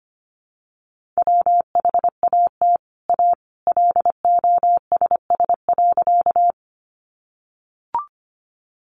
Morse code, heard as a keyed tone, about 25 wpm, 700 Hz. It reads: WHAT A LOSS.
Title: Sentences from Top 1400 Words 25wpm